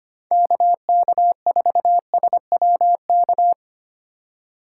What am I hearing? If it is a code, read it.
Morse code, heard as a keyed tone, about 29 wpm, 700 Hz. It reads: KK4SWK
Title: Speed Racing - USA 2x3 Call Signs 25wpm